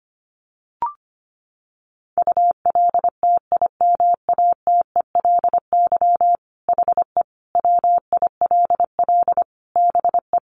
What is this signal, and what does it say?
Morse code, heard as a keyed tone, about 25 wpm, 700 Hz. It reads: ULTIMATELY HE WILL BE
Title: Sentences from Top 1200 Words 25wpm